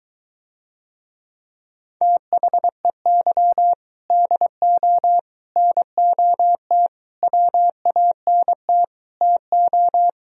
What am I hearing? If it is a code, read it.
Morse code, heard as a keyed tone, about 24 wpm, 700 Hz. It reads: THEY DO NOT WANT TO